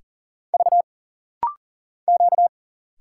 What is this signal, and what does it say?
Morse code, heard as a keyed tone, about 40 wpm, 700 Hz. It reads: V Q
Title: Single Letters - Rapid-Fire 40wpm